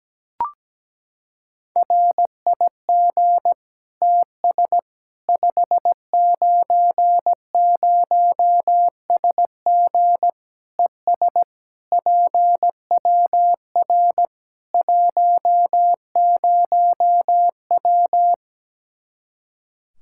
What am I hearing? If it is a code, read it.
Morse code, heard as a keyed tone, about 17 wpm, 700 Hz. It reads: RIG TS 590SG ES PWR 10W
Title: Practice QSO - Exchange 2 - Info - 17wpm